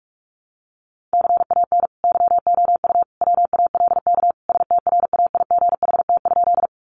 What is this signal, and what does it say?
Morse code, heard as a keyed tone, about 44 wpm, 700 Hz. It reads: CAN YOU WALK STRAIGHT?
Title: Sentences from Top 1000 Words 45wpm